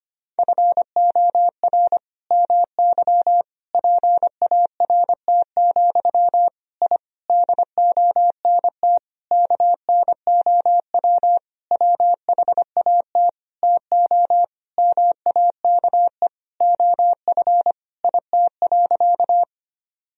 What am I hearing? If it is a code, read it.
Morse code, heard as a keyed tone, about 25 wpm, 700 Hz. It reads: FOR MY PART, I DONT KNOW WHAT TO MAKE OF IT.
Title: Sentences from Top 1000 Words - Review Part 1 - 25wpm